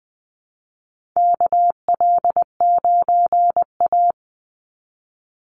Morse code, heard as a keyed tone, about 19 wpm, 700 Hz. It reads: KL9A
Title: Call Signs - 20wpm